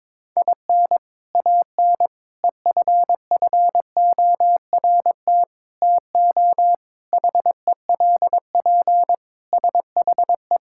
Morse code, heard as a keyed tone, about 22 wpm, 700 Hz. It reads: IN AN EFFORT TO HELP SHE
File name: practice / 22wpm / Sentences with 4-Word Phrases - 22wpm